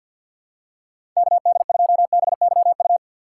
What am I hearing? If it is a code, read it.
Morse code, heard as a keyed tone, about 53 wpm, 700 Hz. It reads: KD2BYU
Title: USA Call Signs - 6 Characters - 4x Character Spacing - 50wpm